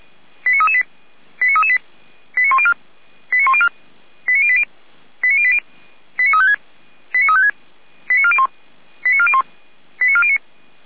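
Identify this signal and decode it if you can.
Five-tone selective call (ZVEI), heard as quick runs of five tones, tones of about 80 ms. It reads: 89338 89338 89104 89104 89080 89080 89356 89356 89401 89401 89409
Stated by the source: ZVEI five-tone selective call received by SDR.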